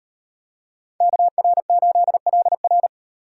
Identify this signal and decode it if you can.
Morse code, heard as a keyed tone, about 35 wpm, 700 Hz. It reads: KR8LR